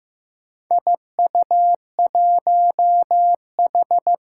Morse code, heard as a keyed tone, about 15 wpm, 700 Hz. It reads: IU1H
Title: Four Challenging Letter-Number Combinations 15wpm